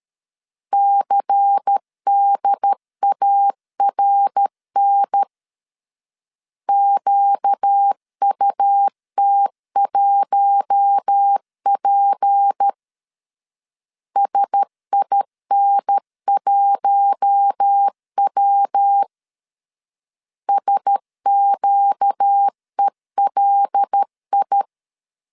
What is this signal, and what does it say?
Morse code, heard as a keyed tone, about 13 wpm, 800 Hz. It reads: CDARN QUT1P SIN1W SQELI
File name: LEZIONI / 6 PKX0 / 6 P 12wpm 2033